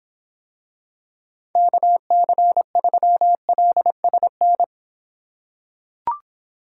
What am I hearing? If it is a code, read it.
Morse code, heard as a keyed tone, about 26 wpm, 700 Hz. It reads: KC3LSN